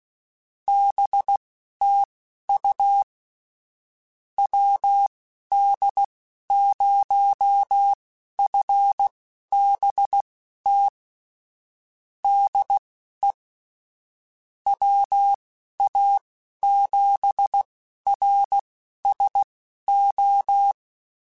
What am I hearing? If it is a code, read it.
Morse code, heard as a keyed tone, about 16 wpm, spaced as for 11 wpm, 800 Hz. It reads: BTU WD0FBT DE WA7RSO
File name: QSOs / QSOs-mp3s / WD0FBT 1@15